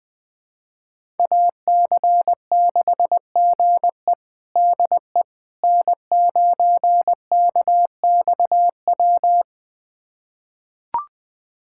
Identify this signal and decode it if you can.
Morse code, heard as a keyed tone, about 20 wpm, 700 Hz. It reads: AC6GE DE N9KXW
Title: USA Callsign DE Callsign - 5 Characters - 20wpm